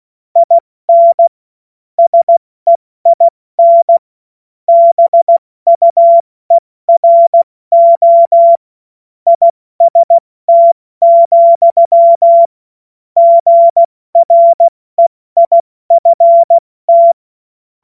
Morse code, heard as a keyed tone, about 16 wpm, 680 Hz. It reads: IN SEIN BUERO IST, GREIFT
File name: Morsekurs / Witze / Witz 06. Dezember 2022 70 F80